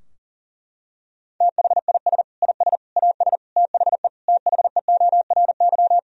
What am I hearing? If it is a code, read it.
Morse code, heard as a keyed tone, about 40 wpm, 700 Hz. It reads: THIS IS AS THE THEORY
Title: Sentences from Top 1000 Words - Review Part 3 - 40wpm